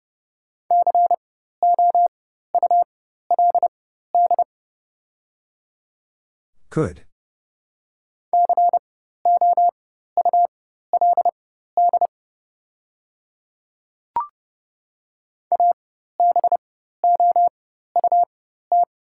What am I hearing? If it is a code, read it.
Morse code, heard as a keyed tone, about 30 wpm, spaced as for 14 wpm, 700 Hz. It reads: COULD COULD ABOUT
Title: 4x Character Spacing - Top 100 Words - 30wpm